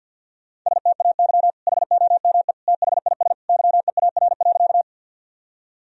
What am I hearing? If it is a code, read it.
Morse code, heard as a keyed tone, about 51 wpm, 700 Hz. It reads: STAY HOME THIS YEAR.
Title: Sentences from Top 1000 Words - Review Part 1 - 50wpm